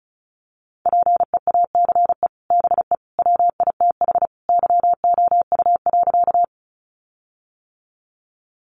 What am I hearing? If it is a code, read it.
Morse code, heard as a keyed tone, about 34 wpm, 700 Hz. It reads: PEACE BE WITH YOU.
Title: Sentences from Top 1500 Words 35wpm